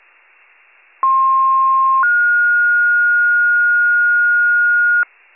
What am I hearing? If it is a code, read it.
Two-tone page, heard as a one-second tone, then a three-second tone, about 1060 then 1430 Hz, with noise. It is heard from 1.0 to 5.0 s.